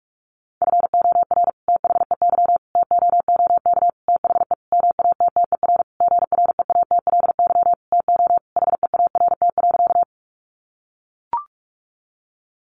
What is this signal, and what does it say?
Morse code, heard as a keyed tone, about 44 wpm, 700 Hz. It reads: FOR THEY TOOK THE MATTER GREATLY TO HEART.
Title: Sentences from Top 1400 Words 45wpm